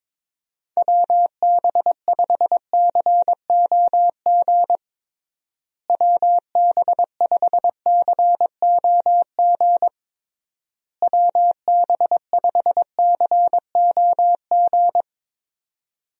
Morse code, heard as a keyed tone, about 22 wpm, 700 Hz. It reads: WB5COG WB5COG WB5COG
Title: USA Call Signs - 6 Characters - Repeated 3x - 22wpm